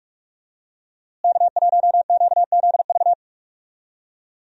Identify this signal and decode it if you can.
Morse code, heard as a keyed tone, about 46 wpm, 700 Hz. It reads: K1QZV
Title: USA Call Signs - 5 Characters - Repeated 2x - 45wpm